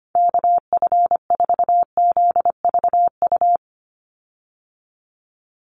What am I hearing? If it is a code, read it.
Morse code, heard as a keyed tone, about 24 wpm, 700 Hz. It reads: KF4ZVU
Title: USA Call Signs 25wpm